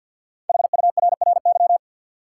Morse code, heard as a keyed tone, about 50 wpm, 700 Hz. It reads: HURRY
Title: Five Letter Words - 50wpm